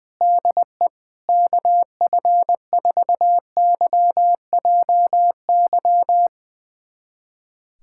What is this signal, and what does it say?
Morse code, heard as a keyed tone, about 20 wpm, 700 Hz. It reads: DE KF4YJY
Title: USA Callsign DE Callsign - 6 Characters - 20wpm